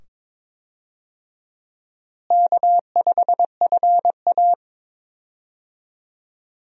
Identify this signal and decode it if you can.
Morse code, heard as a keyed tone, about 22 wpm, 700 Hz. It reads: K5FA
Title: USA Call Signs - 4 Characters - Repeated 2x - 22wpm